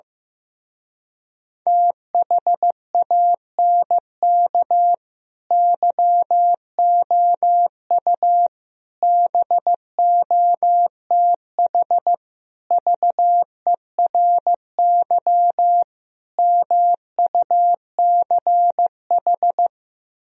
Morse code, heard as a keyed tone, about 15 wpm, 700 Hz. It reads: THANK YOU BOTH VERY MUCH
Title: Speed-Racing - Sets of 5 Words 15wpm